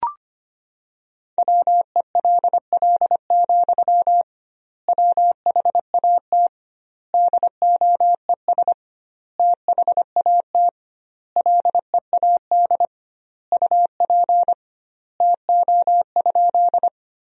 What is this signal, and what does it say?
Morse code, heard as a keyed tone, about 25 wpm, 700 Hz. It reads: WELL, WHAT DOES THAT LEAD UP TO?
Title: 2x Word-Spacing - Sentences from Top 500 Words - 25wpm